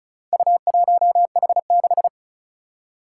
Morse code, heard as a keyed tone, about 36 wpm, 700 Hz. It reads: U1H6
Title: Four Challenging Letter-Number Combinations 35wpm